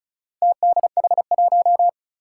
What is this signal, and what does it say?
Morse code, heard as a keyed tone, about 36 wpm, 700 Hz. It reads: TDH1
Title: Four Challenging Letter-Number Combinations 35wpm